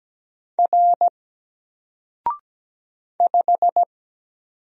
Morse code, heard as a keyed tone, about 17 wpm, 700 Hz. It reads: R 5
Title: Single Letter-Number Alt 17wpm